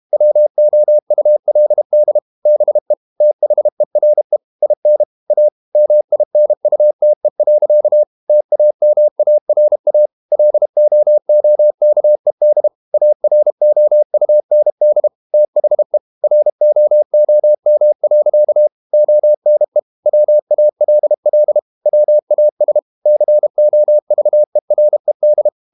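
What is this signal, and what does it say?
Morse code, heard as a keyed tone, about 32 wpm, 600 Hz. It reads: WOULD BE THERE IN A MINUTE. TAMARA LOOKED AROUND THE ROOM. ONE WALL WAS COVERED
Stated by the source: Morse code practice at 32 wpm from CW med Gnister, reading Ken Follett's novel Never in English.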